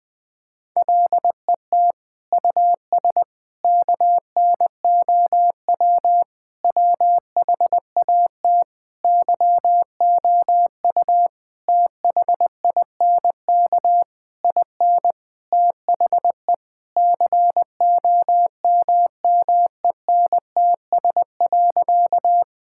Morse code, heard as a keyed tone, about 20 wpm, 700 Hz. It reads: LET US KNOW WHAT YOU THINK IN THE COMMENTS.
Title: Sentences with 8-Word Phrases - 20wpm